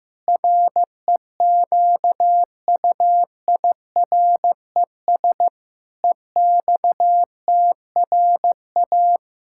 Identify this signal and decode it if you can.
Morse code, heard as a keyed tone, about 15 wpm, 700 Hz. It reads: REQUIRES EXTRA
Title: Pangram Challenge v2 - 15wpm